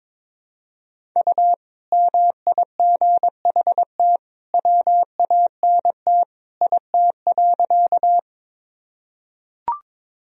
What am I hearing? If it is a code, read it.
Morse code, heard as a keyed tone, about 22 wpm, 700 Hz. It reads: U MIGHT WANT IT.
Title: Sentences from Top 300 Words and Abbreviations 22wpm